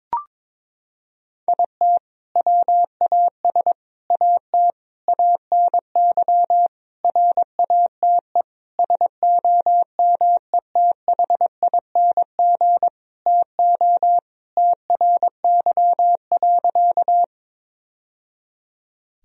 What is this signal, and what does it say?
Morse code, heard as a keyed tone, about 22 wpm, 700 Hz. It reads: IT WAS AT ANY RATE SOMETHING TO TRY.
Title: Sentences from Top 1000 Words - Review Part 4 - 22wpm